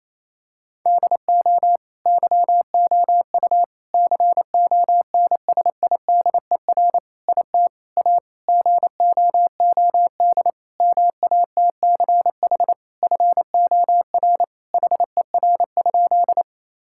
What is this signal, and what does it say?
Morse code, heard as a keyed tone, about 27 wpm, 700 Hz. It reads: DO YOU CONSIDER IT A GOOD MATCH FOR HER?
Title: Sentences from Top 1600 Words 28wpm